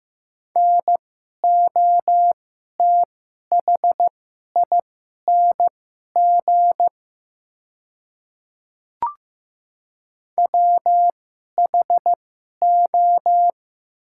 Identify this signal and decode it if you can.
Morse code, heard as a keyed tone, about 15 wpm, spaced as for 11 wpm, 700 Hz. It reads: NOTHING WHO
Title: 2x Character Spacing - Top 500 Words - 15wpm